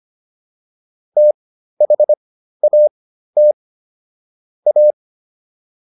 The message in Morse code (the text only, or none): THAT A